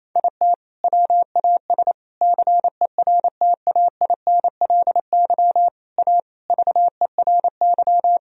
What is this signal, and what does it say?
Morse code, heard as a keyed tone, about 27 wpm, 700 Hz. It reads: IT WAS CERTAINLY A VERY
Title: Sentences from Top 1000 Words - Review Part 3 - 28wpm